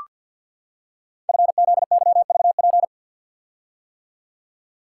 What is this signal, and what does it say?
Morse code, heard as a keyed tone, about 46 wpm, 690 Hz. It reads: FZYVP
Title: Code Groups - 5 Letter:Numbers - 50wpm